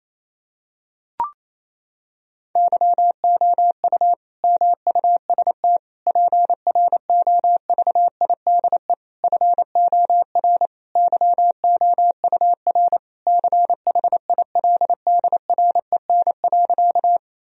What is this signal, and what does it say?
Morse code, heard as a keyed tone, about 27 wpm, 700 Hz. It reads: YOU MUST PROVIDE FOR YOUR CHILDREN.
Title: Sentences from Top 1000 Words - Review Part 3 - 28wpm